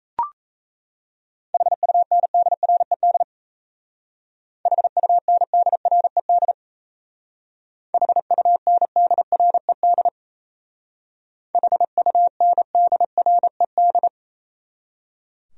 Morse code, heard as a keyed tone, about 36 wpm, 700 Hz. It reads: HUNDRED HUNDRED HUNDRED HUNDRED